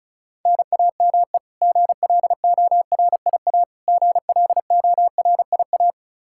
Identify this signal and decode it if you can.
Morse code, heard as a keyed tone, about 36 wpm, 700 Hz. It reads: NAME GLORIA GLORIA
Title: K1USN SST QSO Practice 35wpm